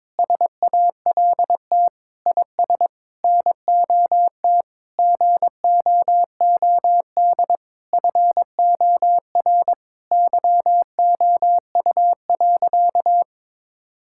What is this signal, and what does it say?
Morse code, heard as a keyed tone, about 21 wpm, 700 Hz. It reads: SALT IS NOT GOOD FOR YOU.
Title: Sentences from Top 1000 Words - Review Part 5 - 22wpm